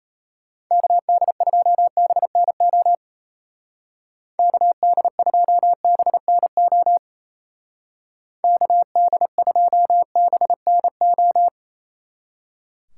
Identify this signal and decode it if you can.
Morse code, heard as a keyed tone, about 33 wpm, 700 Hz. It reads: KD2BNO KD2BNO KD2BNO
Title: Speed Racing - USA 2x3 Call Signs 28wpm